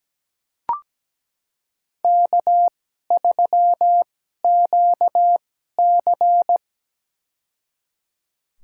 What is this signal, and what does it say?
Morse code, heard as a keyed tone, about 17 wpm, spaced as for 12 wpm, 700 Hz. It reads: K3QC
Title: USA Call Signs - 4 Characters - 2x Character Spacing - 17wpm